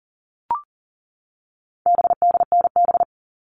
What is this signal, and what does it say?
Morse code, heard as a keyed tone, about 40 wpm, 700 Hz. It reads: BDNB
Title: All Cut Numbers - 4-Digit 40wpm